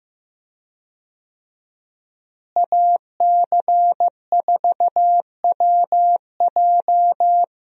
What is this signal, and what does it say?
Morse code, heard as a keyed tone, about 15 wpm, 700 Hz. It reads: AC4WJ
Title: USA Call Signs - 5 Characters - Repeated 2x - 15wpm